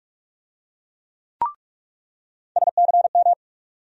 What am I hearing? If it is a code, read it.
Morse code, heard as a keyed tone, about 46 wpm, 700 Hz. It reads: SCM
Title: USA Call Sign Postfixes - 3 Letters 45wpm